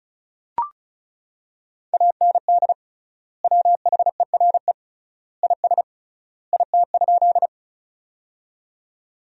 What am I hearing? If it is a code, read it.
Morse code, heard as a keyed tone, about 34 wpm, 700 Hz. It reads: AND WHERE IS IT?
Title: 3x Word-Spacing - Sentences from Top 200 Words - 35wpm